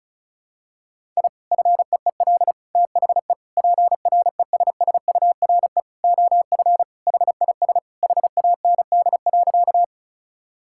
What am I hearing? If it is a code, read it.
Morse code, heard as a keyed tone, about 36 wpm, 700 Hz. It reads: I FEEL THE PRESSURE OF HIS HAND.
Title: Sentences from Top 1000 Words - Review Part 6 - 35wpm